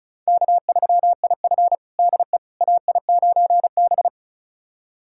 Morse code, heard as a keyed tone, about 36 wpm, 700 Hz. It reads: K3IF DE AI9B
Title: USA Callsign DE Callsign - 4 Characters - 35wpm